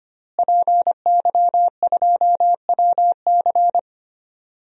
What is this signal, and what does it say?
Morse code, heard as a keyed tone, about 25 wpm, 700 Hz. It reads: PY2WC
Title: DXCC-100 Call Signs - 5 Characters - Repeated 2x - 25wpm